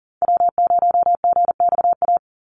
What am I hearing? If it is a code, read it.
Morse code, heard as a keyed tone, about 40 wpm, 700 Hz. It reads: W0GXA
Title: Call Signs Repeated 2x - 40wpm